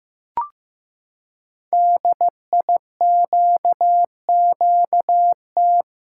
Morse code, heard as a keyed tone, about 15 wpm, 700 Hz. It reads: DIQQT
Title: Code Groups - 5 Letters - 15wpm